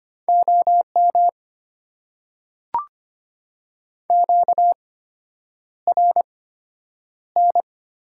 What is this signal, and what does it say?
Morse code, heard as a keyed tone, about 25 wpm, spaced as for 9 wpm, 700 Hz. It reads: OM QRN
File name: Top 90 QSO Elements - 8x Character Spacing - 25wpm